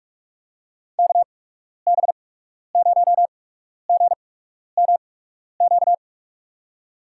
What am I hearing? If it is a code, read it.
Morse code, heard as a keyed tone, about 47 wpm, spaced as for 12 wpm, 700 Hz. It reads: KB0GMQ
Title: USA Call Signs - 6 Characters - 8x Character Spacing - 45wpm